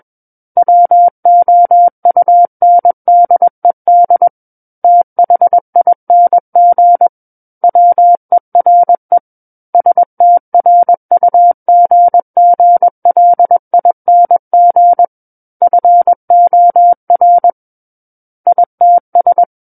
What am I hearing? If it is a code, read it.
Morse code, heard as a keyed tone, about 21 wpm, 700 Hz. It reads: WOUNDED THING WERE STRUGGLING FOR ITS
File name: Book One / 12-Chapter 12 - 21 WPM